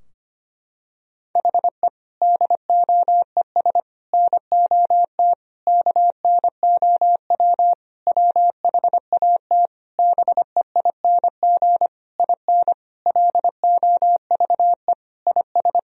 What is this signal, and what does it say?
Morse code, heard as a keyed tone, about 25 wpm, 700 Hz. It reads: HE DOES NOT KNOW WHAT BEING IN LOVE IS
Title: Sentences from Top 1000 Words - Review Part 3 - 25wpm